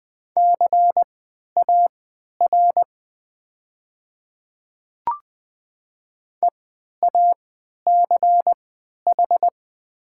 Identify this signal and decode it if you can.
Morse code, heard as a keyed tone, about 20 wpm, spaced as for 11 wpm, 700 Hz. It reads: CAR EACH
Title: 3x Character Spacing - Top 300 Words - 20wpm